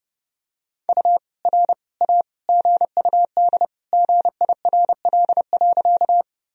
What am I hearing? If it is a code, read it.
Morse code, heard as a keyed tone, about 30 wpm, 700 Hz. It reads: U R A GUD GIRL.
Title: Sentences from Top 400 Words and Abbreviations 30wpm